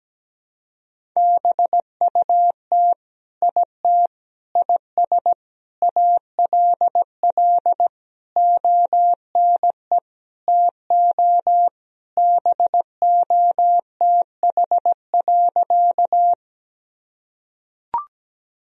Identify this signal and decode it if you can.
Morse code, heard as a keyed tone, about 17 wpm, 700 Hz. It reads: BUT IT IS ALL ONE TO BOTH.
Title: Sentences from Top 1000 Words - Review Part 1 - 17wpm